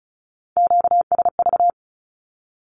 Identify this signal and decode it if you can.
Morse code, heard as a keyed tone, about 34 wpm, 700 Hz. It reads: QSV
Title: Q-Codes 35wpm